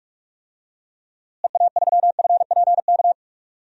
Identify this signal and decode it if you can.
Morse code, heard as a keyed tone, about 42 wpm, 700 Hz. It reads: EA3FPK
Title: DXCC-100 Call Signs - 6 Characters - Repeated 2x - 45wpm